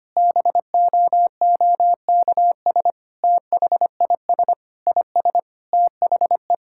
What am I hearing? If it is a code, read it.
Morse code, heard as a keyed tone, about 25 wpm, 700 Hz. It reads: BOOKS THIS IS THE
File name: practice / 25wpm / Sentences with 3-Word Phrases - 25wpm